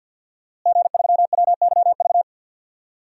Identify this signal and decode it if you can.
Morse code, heard as a keyed tone, about 52 wpm, 700 Hz. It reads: G3WYV
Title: DXCC-100 Call Signs - 5 Characters - Repeated 3x - 50wpm